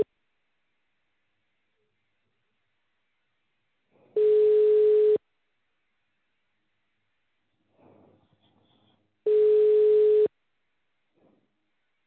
Ringback tone (Europe, 425 Hz).